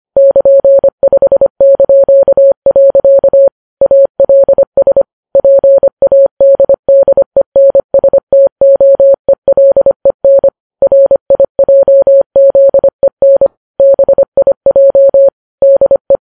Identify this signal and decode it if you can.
Morse code, heard as a keyed tone, about 25 wpm, 550 Hz. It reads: <KN>5). ALS PADDENSTOELEN RIJZEN BIJ DE